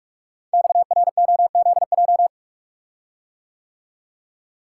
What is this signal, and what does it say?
Morse code, heard as a keyed tone, about 44 wpm, 690 Hz. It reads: XROZJ
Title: Code Groups - 5 Letters - 45wpm